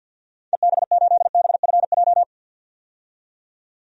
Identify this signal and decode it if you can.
Morse code, heard as a keyed tone, about 47 wpm, 700 Hz. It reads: EB8BFJ
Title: DXCC-100 Call Signs - 6 Characters - 50wpm